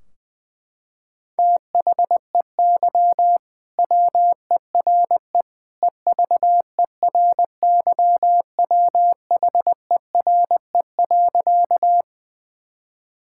Morse code, heard as a keyed tone, about 20 wpm, 700 Hz. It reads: THEY WERE EVERYWHERE.
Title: Sentences from Top 1600 Words 20wpm